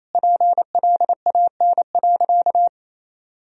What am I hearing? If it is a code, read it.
Morse code, heard as a keyed tone, about 28 wpm, 700 Hz. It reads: PLAN.
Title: Sentences from Top 1000 Words - Review Part 2 - 28wpm